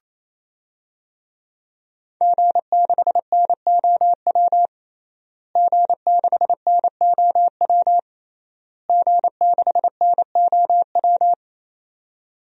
Morse code, heard as a keyed tone, about 27 wpm, 700 Hz. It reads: G6NOW G6NOW G6NOW
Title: DXCC-100 Call Signs - 5 Characters - Repeated 3x - 28wpm